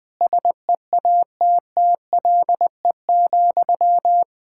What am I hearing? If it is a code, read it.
Morse code, heard as a keyed tone, about 21 wpm, 700 Hz. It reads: SEATTLE,
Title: Practice QSO - Exchange 1 - Beginning - 20wpm